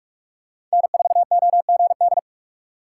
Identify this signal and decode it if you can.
Morse code, heard as a keyed tone, about 46 wpm, 690 Hz. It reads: N4OGD
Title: USA Call Signs - 5 Characters - 2x Character Spacing - 45wpm